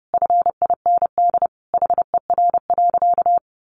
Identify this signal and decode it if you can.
Morse code, heard as a keyed tone, about 30 wpm, 700 Hz. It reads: FIND HER.
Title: Sentences from Top 1300 Words 30wpm